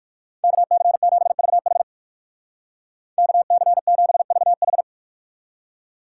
Morse code, heard as a keyed tone, about 50 wpm, 700 Hz. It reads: KC7VH KC7VH
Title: Speed Racing - USA 2x2 Call Signs 45wpm